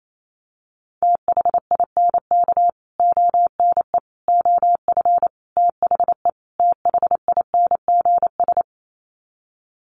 Morse code, heard as a keyed tone, about 28 wpm, 700 Hz. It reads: THINK ONE OF THE THINGS
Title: Sets of 5 Words Encore2 28wpm